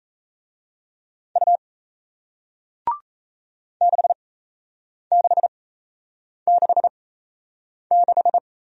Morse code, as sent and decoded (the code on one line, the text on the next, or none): ..- / -.... -.... -.... -....
U 6666